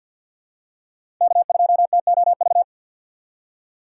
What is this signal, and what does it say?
Morse code, heard as a keyed tone, about 48 wpm, 700 Hz. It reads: K2TOV
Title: USA Call Signs - 5 Characters - 4x Character Spacing - 50wpm